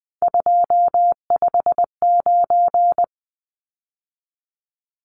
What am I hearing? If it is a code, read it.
Morse code, heard as a keyed tone, about 20 wpm, 700 Hz. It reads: 259
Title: Numbers - 3-Digit 20wpm